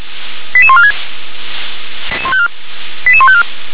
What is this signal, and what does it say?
Five-tone selective call (ZVEI), heard as quick runs of five tones, tones of about 70 ms. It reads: 88146 88154